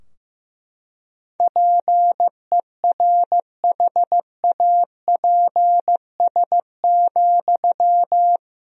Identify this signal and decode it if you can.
Morse code, heard as a keyed tone, about 15 wpm, 700 Hz. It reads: PERHAPS,
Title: Sentences from Top 1000 Words - Review Part 4 - 15wpm